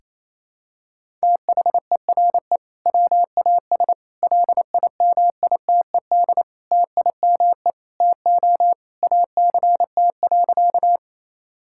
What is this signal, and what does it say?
Morse code, heard as a keyed tone, about 28 wpm, 700 Hz. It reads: THERE WAS LIMITED TIME TO ACT.
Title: Sentences from Top 1000 Words - Review Part 6 - 28wpm